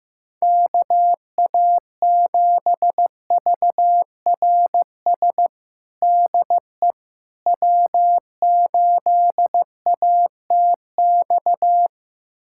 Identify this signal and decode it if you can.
Morse code, heard as a keyed tone, about 15 wpm, 700 Hz. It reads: KA7VRS DE W8ATX
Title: USA Callsign DE Callsign 15wpm